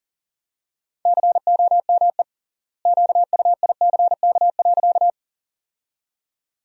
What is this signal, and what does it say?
Morse code, heard as a keyed tone, about 40 wpm, 700 Hz. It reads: COME QUICK.
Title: 3x Word-Spacing - Sentences from Top 500 Words - 40wpm